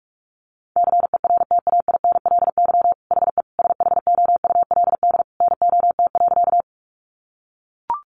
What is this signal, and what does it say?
Morse code, heard as a keyed tone, about 44 wpm, 700 Hz. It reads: CERTAINLY HE SHOULD NOT.
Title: Sentences from Top 900 Words 45wpm